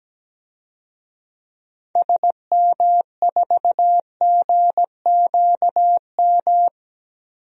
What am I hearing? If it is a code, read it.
Morse code, heard as a keyed tone, about 17 wpm, 700 Hz. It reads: SM4GQM
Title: DXCC-100 Call Signs - 6 Characters - 17wpm